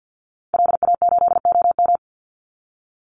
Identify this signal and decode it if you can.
Morse code, heard as a keyed tone, about 50 wpm, 700 Hz. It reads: LA8OM
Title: Call Signs - 50wpm